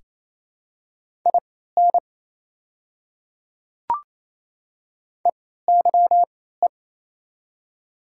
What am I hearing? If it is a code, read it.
Morse code, heard as a keyed tone, about 28 wpm, spaced as for 16 wpm, 700 Hz. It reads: IN EYE
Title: 3x Character Spacing - Top 300 Words - 28wpm